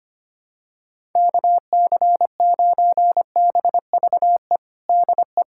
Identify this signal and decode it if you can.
Morse code, heard as a keyed tone, about 25 wpm, 700 Hz. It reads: KC9BVE DE
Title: USA Callsign DE Callsign - 6 Characters - 25wpm